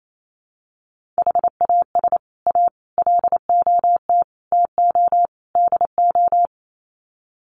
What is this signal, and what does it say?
Morse code, heard as a keyed tone, about 28 wpm, 700 Hz. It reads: HAS A LOT TO DO
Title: Sets of 5 Words Encore2 28wpm